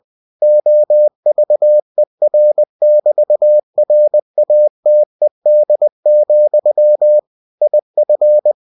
Morse code, heard as a keyed tone, about 20 wpm, 600 Hz. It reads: OVER<BT>RATED, IF
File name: cw-books / winnie-the-pooh / winnie-20wpm-page-0109